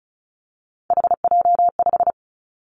Morse code, heard as a keyed tone, about 34 wpm, 700 Hz. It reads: HJ5
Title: Callsign Prefixes for Top 100 DXCC - 35wpm